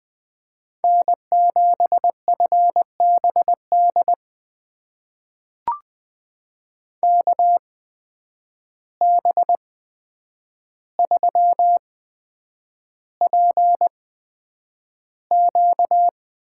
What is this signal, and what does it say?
Morse code, heard as a keyed tone, about 20 wpm, spaced as for 9 wpm, 700 Hz. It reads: N7FBD KB3PQ
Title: USA Call Signs - 5 Characters - 8x Character Spacing - 20wpm